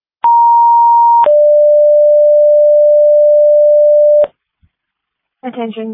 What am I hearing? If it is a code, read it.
Two-tone page, heard as a one-second tone, then a three-second tone, about 940 then 590 Hz, with speech. It is heard from 0.2 to 4.2 s.